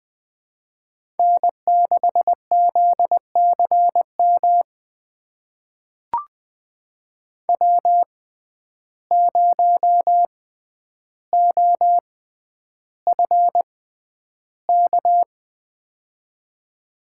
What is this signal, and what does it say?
Morse code, heard as a keyed tone, about 20 wpm, spaced as for 10 wpm, 700 Hz. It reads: N6ZCM W0OFK
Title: USA Call Signs - 5 Characters - 6x Character Spacing - 20wpm